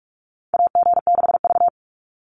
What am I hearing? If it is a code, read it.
Morse code, heard as a keyed tone, about 44 wpm, 700 Hz. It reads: AG6V
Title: NA SOTA Chaser Callsigns 45wpm